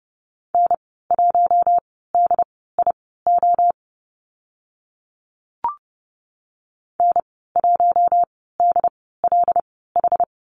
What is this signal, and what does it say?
Morse code, heard as a keyed tone, about 30 wpm, spaced as for 17 wpm, 700 Hz. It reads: N1DIO N1DLH
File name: Lesson 14g - Number 2 - Call Signs 30wpm